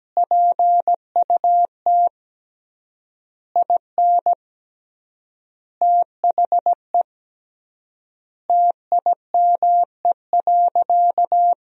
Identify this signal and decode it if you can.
Morse code, heard as a keyed tone, about 17 wpm, 700 Hz. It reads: PUT IN THE TIME.
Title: 3x Word-Spacing - Sentences from Top 200 Words - 17wpm